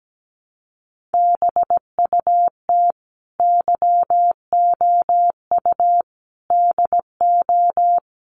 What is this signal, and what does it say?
Morse code, heard as a keyed tone, about 17 wpm, 700 Hz. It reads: BUT YOU DO
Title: Sentences from Top 1100 Words 17wpm